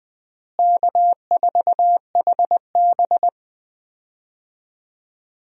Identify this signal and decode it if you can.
Morse code, heard as a keyed tone, about 20 wpm, 700 Hz. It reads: K4HB
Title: USA Call Signs - 4 Characters - Repeated 3x - 20wpm